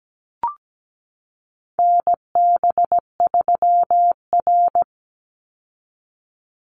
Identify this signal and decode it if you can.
Morse code, heard as a keyed tone, about 17 wpm, 700 Hz. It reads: NB3R
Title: Call Signs - 17wpm